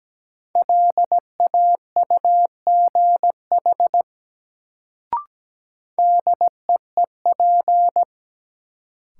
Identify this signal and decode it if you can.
Morse code, heard as a keyed tone, about 17 wpm, 700 Hz. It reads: LAUGH DEEP
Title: Sentences from Top 500 Words - New Word Drill - 17wpm